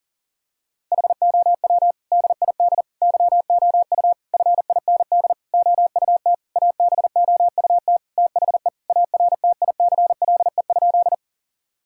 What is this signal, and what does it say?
Morse code, heard as a keyed tone, about 40 wpm, 700 Hz. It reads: HOW DID YOU FIND OUT ABOUT THE ARTICLE?
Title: Sentences from Top 1000 Words - Review Part 5 - 40wpm